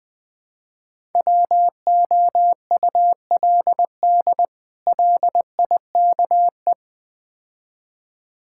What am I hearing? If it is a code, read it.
Morse code, heard as a keyed tone, about 20 wpm, 700 Hz. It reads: WOULD LIKE
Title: Build A Phrase - 3-Word Phrases - 20wpm